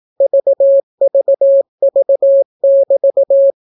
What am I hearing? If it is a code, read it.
Morse code, heard as a keyed tone, about 18 wpm, 550 Hz. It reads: VVV<BT>